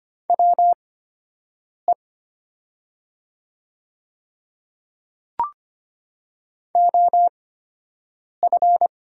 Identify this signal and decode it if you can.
Morse code, heard as a keyed tone, about 25 wpm, spaced as for 7 wpm, 700 Hz. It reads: WE OF
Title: 8x Character Spacing - Top 100 Words - 25wpm